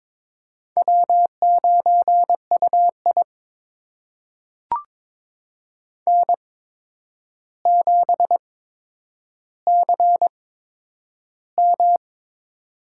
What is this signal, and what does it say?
Morse code, heard as a keyed tone, about 22 wpm, spaced as for 9 wpm, 700 Hz. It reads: W9UI N7CM
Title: USA Call Signs - 4 Characters - 8x Character Spacing - 22wpm